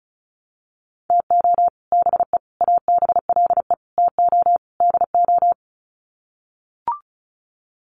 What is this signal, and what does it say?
Morse code, heard as a keyed tone, about 34 wpm, 700 Hz. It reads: TO BE ABLE TO DO
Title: Sets of 5 Words 35wpm